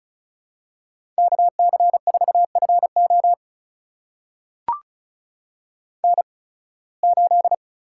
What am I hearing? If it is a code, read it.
Morse code, heard as a keyed tone, about 34 wpm, spaced as for 23 wpm, 700 Hz. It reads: KC4FO N8
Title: USA Call Signs - 5 Characters - 8x Character Spacing - 35wpm